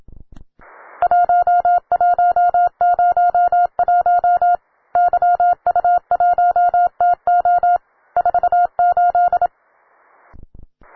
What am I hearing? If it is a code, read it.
Morse code, heard as a keyed tone, about 27 wpm, 710 Hz. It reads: 1101 YU1TO 48
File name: wae / cw beginner / ua9cdv cw